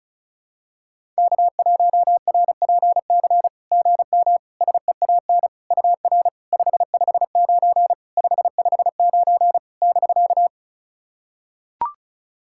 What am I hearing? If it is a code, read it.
Morse code, heard as a keyed tone, about 34 wpm, 690 Hz. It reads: K1RPC GM SEAN UR 559 559 <BK>
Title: SOTA QSO Practice 35wpm